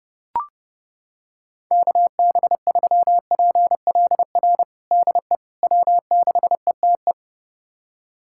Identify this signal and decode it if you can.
Morse code, heard as a keyed tone, about 32 wpm, 700 Hz. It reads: KB3PLR DE W6ETE
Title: USA Callsign DE Callsign 30wpm